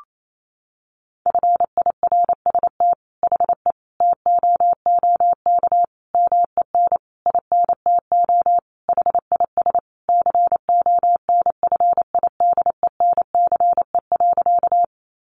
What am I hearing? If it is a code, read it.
Morse code, heard as a keyed tone, about 28 wpm, 700 Hz. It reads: FIRST HE TOOK MEN INTO HIS CONFIDENCE.
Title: Sentences from Top 1500 Words 28wpm